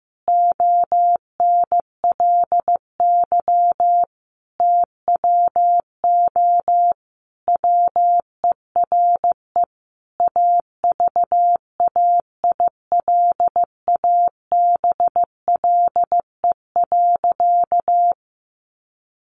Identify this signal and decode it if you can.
Morse code, heard as a keyed tone, about 15 wpm, 700 Hz. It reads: ONLY TWO WERE AVAILABLE.
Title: Sentences from Top 1100 Words 15wpm